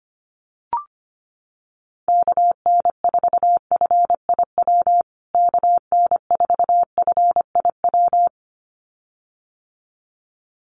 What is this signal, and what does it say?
Morse code, heard as a keyed tone, about 25 wpm, 700 Hz. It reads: KN4FIW KN4FIW
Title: USA Call Signs Repeated 2x 25wpm